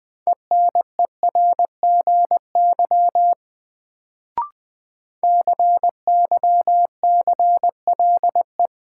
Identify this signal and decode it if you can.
Morse code, heard as a keyed tone, about 20 wpm, 700 Hz. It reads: ENERGY CYCLE
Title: Sentences from Top 900 Words - New Word Drill - 20wpm